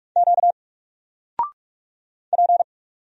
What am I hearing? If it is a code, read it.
Morse code, heard as a keyed tone, about 47 wpm, 700 Hz. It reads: Q P
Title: Single Character (Letter:Number:Punctuation:Prosign) Alt 45wpm